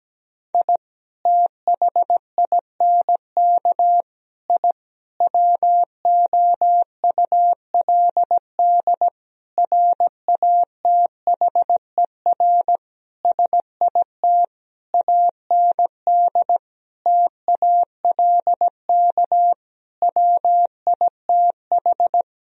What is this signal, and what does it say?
Morse code, heard as a keyed tone, about 17 wpm, 700 Hz. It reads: I THINK I WOULD RATHER SIT AND TALK WITH
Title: Sentences from Top 1000 Words - Review Part 4 - 17wpm